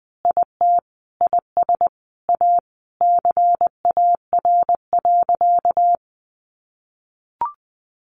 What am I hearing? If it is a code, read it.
Morse code, heard as a keyed tone, about 20 wpm, 700 Hz. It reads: IT IS A CAR.
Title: Easy Phrases 20wpm